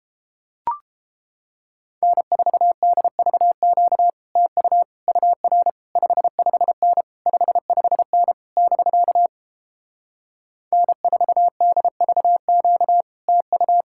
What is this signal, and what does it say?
Morse code, heard as a keyed tone, about 31 wpm, 700 Hz. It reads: N4DVQ TU UR 55N 55N <BK> N4DVQ TU